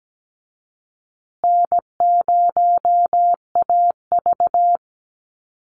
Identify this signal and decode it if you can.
Morse code, heard as a keyed tone, about 17 wpm, 700 Hz. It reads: N0AV
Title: Easy Call Signs Repeated 3x - 17wpm